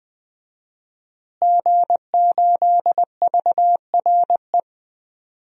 Morse code, heard as a keyed tone, about 20 wpm, 700 Hz. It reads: G8VRE
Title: DXCC-100 Call Signs - 5 Characters - Repeated 3x - 20wpm